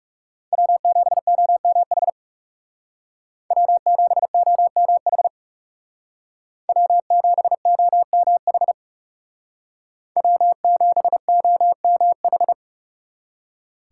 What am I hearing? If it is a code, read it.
Morse code, heard as a keyed tone, about 38 wpm, 700 Hz. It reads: W7OMH W7OMH W7OMH W7OMH